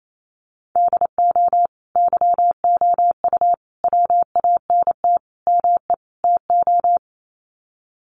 Morse code, heard as a keyed tone, about 28 wpm, 700 Hz. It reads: DO YOU WANT ME TO
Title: Sets of 5 Words 28wpm